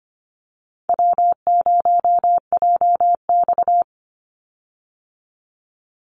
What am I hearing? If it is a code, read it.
Morse code, heard as a keyed tone, about 25 wpm, 700 Hz. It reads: W0JX
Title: Easy Call Signs Repeated 3x - 25wpm